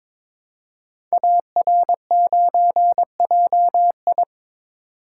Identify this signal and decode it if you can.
Morse code, heard as a keyed tone, about 22 wpm, 700 Hz. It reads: AR9JI
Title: Code Groups - 5 Letter:Numbers - 22wpm